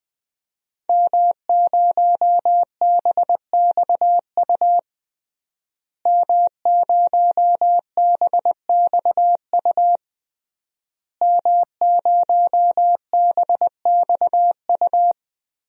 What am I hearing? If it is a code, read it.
Morse code, heard as a keyed tone, about 20 wpm, 700 Hz. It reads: M0BXU M0BXU M0BXU
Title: DXCC-100 Call Signs - 5 Characters - Repeated 3x - 20wpm